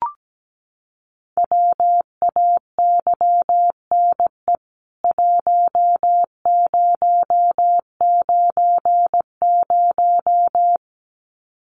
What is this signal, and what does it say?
Morse code, heard as a keyed tone, about 17 wpm, 700 Hz. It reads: WAYNE 1090
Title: CWT Exchange 17wpm